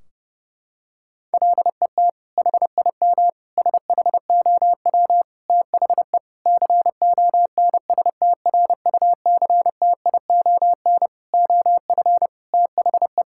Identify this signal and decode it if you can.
Morse code, heard as a keyed tone, about 30 wpm, 700 Hz. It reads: LET HIM SHOW THE CONSTRUCTION OF THE
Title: Sentences from Top 1600 Words 30wpm